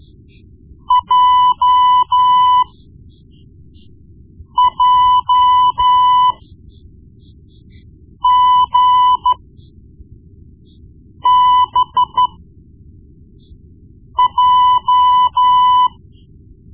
Morse code, heard as a keyed tone, about 13 wpm, spaced as for 4 wpm, 1000 Hz. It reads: JJGBJ
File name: Morse Lessons / Morse Lessons / Tape-1B